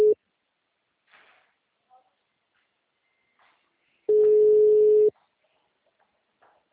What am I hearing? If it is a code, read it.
Ringback tone (Europe, 425 Hz), with speech.
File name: out- -2030- - - .35731